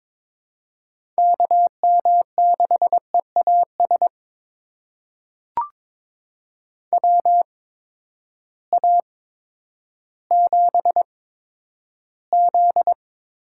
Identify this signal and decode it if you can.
Morse code, heard as a keyed tone, about 22 wpm, spaced as for 11 wpm, 700 Hz. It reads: KM6EAS WA7Z
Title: USA Call Signs - 6 Characters - 8x Character Spacing - 22wpm